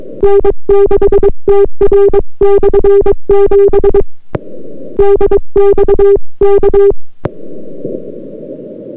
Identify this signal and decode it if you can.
Morse code, heard as a keyed tone, about 22 wpm, 400 Hz. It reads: N6TR/7 DX K